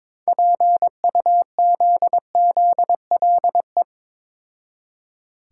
Morse code, heard as a keyed tone, about 22 wpm, 700 Hz. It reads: PUZZLE
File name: Difficult Words with Hints - 22wpm